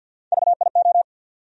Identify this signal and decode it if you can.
Morse code, heard as a keyed tone, about 51 wpm, 700 Hz. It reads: VIO
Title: USA Call Sign Postfixes - 3 Letters 50wpm